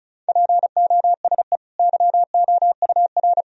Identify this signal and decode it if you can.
Morse code, heard as a keyed tone, about 36 wpm, 700 Hz. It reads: POSE YOUR
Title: Sentences from Top 1000 Words - Review Part 2 - 35wpm